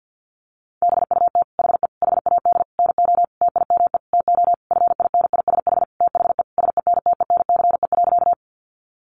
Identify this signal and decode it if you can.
Morse code, heard as a keyed tone, about 50 wpm, 700 Hz. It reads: BUT HE HAD NO TIME TO FINISH THE SENTENCE.
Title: Sentences from Top 1200 Words 50wpm